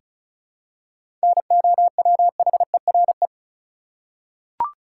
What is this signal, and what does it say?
Morse code, heard as a keyed tone, about 36 wpm, 700 Hz. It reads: NOWHERE
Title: Seven Letter Words - 35wpm